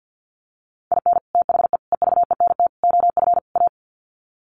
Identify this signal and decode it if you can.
Morse code, heard as a keyed tone, about 50 wpm, 700 Hz. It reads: IN THE EVENT OF A
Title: Sets of 5 Words Encore 50wpm